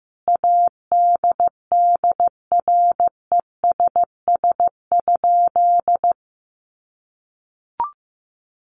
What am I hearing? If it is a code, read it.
Morse code, heard as a keyed tone, about 15 wpm, 700 Hz. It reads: ADDRESS?
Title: Sentences from Top 1000 Words 15wpm